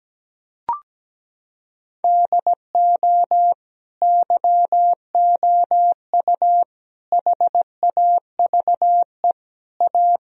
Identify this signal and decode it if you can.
Morse code, heard as a keyed tone, about 17 wpm, 700 Hz. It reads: DO YOU HAVE A
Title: Sentences from Top 1000 Words - Review Part 4 - 17wpm